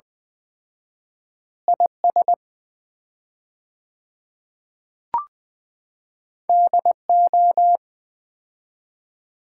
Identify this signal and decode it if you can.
Morse code, heard as a keyed tone, about 21 wpm, 700 Hz. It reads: IS DO
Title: 4x Word-Spacing - Sets of 5 Words - 20wpm